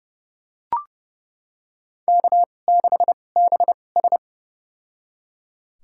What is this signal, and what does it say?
Morse code, heard as a keyed tone, about 30 wpm, spaced as for 22 wpm, 700 Hz. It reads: K6BS
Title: USA Call Signs - 4 Characters - 2x Character Spacing - 30wpm